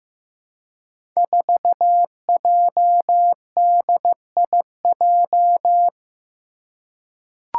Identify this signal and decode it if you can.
Morse code, heard as a keyed tone, about 15 wpm, 700 Hz. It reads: 4JDIJ
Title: Five Challenging Letter-Number Combinations 15wpm